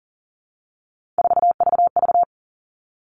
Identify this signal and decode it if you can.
Morse code, heard as a keyed tone, about 40 wpm, 700 Hz. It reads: 4VV
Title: All Cut Numbers - 3-Digit 40wpm